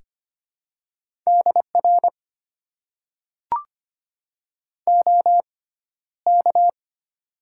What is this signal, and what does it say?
Morse code, heard as a keyed tone, about 25 wpm, spaced as for 13 wpm, 700 Hz. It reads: DR OK
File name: Top 90 QSO Elements - 6x Character Spacing - 25wpm